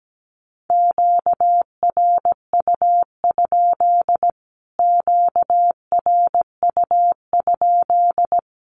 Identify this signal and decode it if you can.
Morse code, heard as a keyed tone, about 17 wpm, 700 Hz. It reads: QRU? QRU?
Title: Q-Codes Repeated 3x 17wpm